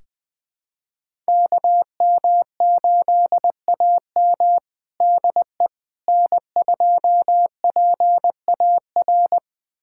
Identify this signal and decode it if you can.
Morse code, heard as a keyed tone, about 20 wpm, 700 Hz. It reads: KM8AM DE N2PAR
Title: USA Callsign DE Callsign - 5 Characters - 20wpm